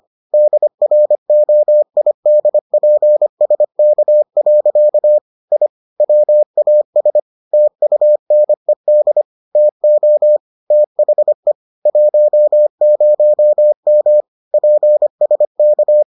Morse code, heard as a keyed tone, about 25 wpm, 600 Hz. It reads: DROIDPSK. I WAS TUNED TO THE 10M PSK